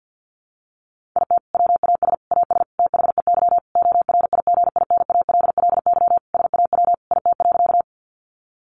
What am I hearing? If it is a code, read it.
Morse code, heard as a keyed tone, about 50 wpm, 700 Hz. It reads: IT WAS AS THEY ORIGINALLY SAW IT.
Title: Sentences from Top 1500 Words 50wpm